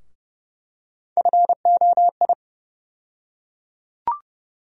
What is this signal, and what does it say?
Morse code, heard as a keyed tone, about 31 wpm, 700 Hz. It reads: FOI
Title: USA Call Sign Postfixes - 3 Letters 30wpm